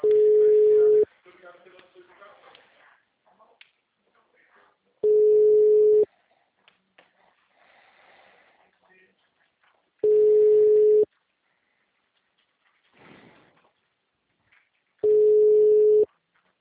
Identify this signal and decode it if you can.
Ringback tone (Europe, 425 Hz), with speech.